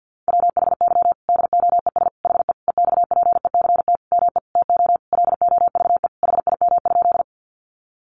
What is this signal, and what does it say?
Morse code, heard as a keyed tone, about 50 wpm, 700 Hz. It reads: WHY DOES HE EXPECT ME TO LOVE HIM?
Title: Sentences from Top 1100 Words 50wpm